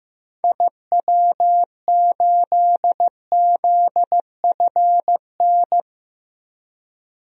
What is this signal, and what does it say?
Morse code, heard as a keyed tone, about 15 wpm, 700 Hz. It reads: IW8ZFN
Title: DXCC-100 Call Signs - 6 Characters - Repeated 3x - 15wpm